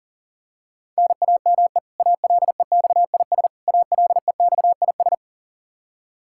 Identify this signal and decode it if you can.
Morse code, heard as a keyed tone, about 40 wpm, 700 Hz. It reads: NAME ALEXIS ALEXIS
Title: K1USN SST QSO Practice 40wpm